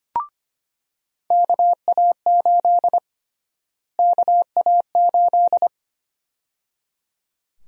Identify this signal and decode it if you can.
Morse code, heard as a keyed tone, about 25 wpm, 700 Hz. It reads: KA8 KA8
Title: USA Call Sign Prefixes - Repeated 2x - 25wpm